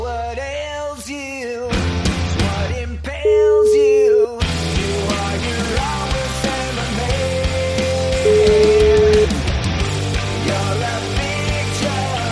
Ringback tone (Europe, 425 Hz), with noise.